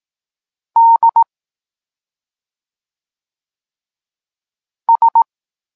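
Morse code, heard as a keyed tone, about 18 wpm, 920 Hz. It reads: D S
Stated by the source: Morse code lesson for learning the letters K and X, from the SP9MOA Niepołomicki Klub Krótkofalowców course.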